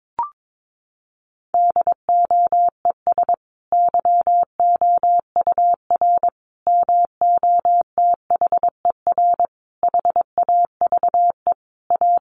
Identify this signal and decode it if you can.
Morse code, heard as a keyed tone, about 22 wpm, 700 Hz. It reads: DOES YOUR MOTHER HAVE A
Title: Sentences from Top 1400 Words 22wpm